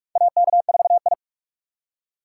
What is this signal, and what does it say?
Morse code, heard as a keyed tone, about 41 wpm, 700 Hz. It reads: AK4I
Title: USA Call Signs - 4 Characters - 3x Character Spacing - 45wpm